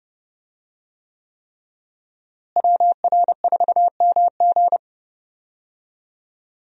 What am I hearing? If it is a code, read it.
Morse code, heard as a keyed tone, about 30 wpm, 700 Hz. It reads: WR4MG
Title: USA Call Signs - 5 Characters - Repeated 3x - 30wpm